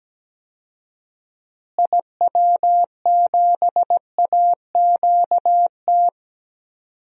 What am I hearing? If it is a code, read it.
Morse code, heard as a keyed tone, about 17 wpm, 700 Hz. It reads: IW7AQT
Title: DXCC-100 Call Signs - 6 Characters - Repeated 2x - 17wpm